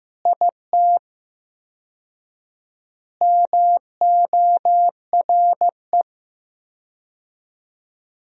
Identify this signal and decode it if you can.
Morse code, heard as a keyed tone, about 16 wpm, 700 Hz. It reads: IT MORE
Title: 4x Word-Spacing - Sentences from Top 400 Words - 15wpm